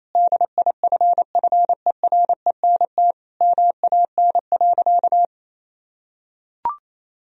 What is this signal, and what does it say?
Morse code, heard as a keyed tone, about 27 wpm, 700 Hz. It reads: DIFFERENT MAN.
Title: Sentences from Top 1000 Words - Review Part 3 - 28wpm